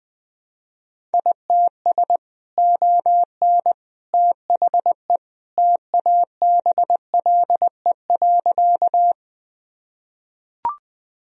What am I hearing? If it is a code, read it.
Morse code, heard as a keyed tone, about 20 wpm, 700 Hz. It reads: ITS ON THE TABLE.
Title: Sentences from Top 1000 Words - Review Part 2 - 20wpm